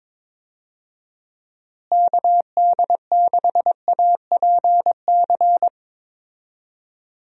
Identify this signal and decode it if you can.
Morse code, heard as a keyed tone, about 22 wpm, 700 Hz. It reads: KD6APC
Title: USA Call Signs - 6 Characters - Repeated 2x - 22wpm